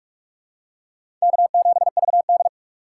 Morse code, heard as a keyed tone, about 46 wpm, 700 Hz. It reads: K7VD
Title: USA Call Signs - 4 Characters - 2x Character Spacing - 45wpm